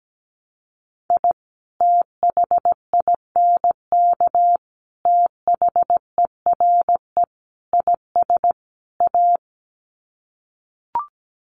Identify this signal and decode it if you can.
Morse code, heard as a keyed tone, about 17 wpm, 700 Hz. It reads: I THINK THERE IS A
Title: Sets of 5 Words 17wpm